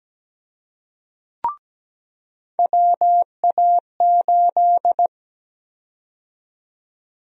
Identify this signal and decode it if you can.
Morse code, heard as a keyed tone, about 17 wpm, 700 Hz. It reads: WA8
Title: USA Call Sign Prefixes - 17wpm